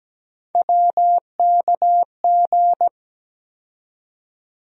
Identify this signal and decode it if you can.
Morse code, heard as a keyed tone, about 17 wpm, 700 Hz. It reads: WKG
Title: USA Call Sign Postfixes - 3 Letters 17wpm